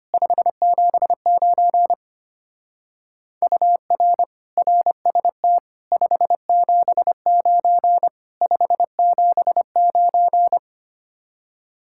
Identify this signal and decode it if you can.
Morse code, heard as a keyed tone, about 31 wpm, 700 Hz. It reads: 579 UR RST 579 579
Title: Speed-Racing Common QSO Phrases Encore - 25wpm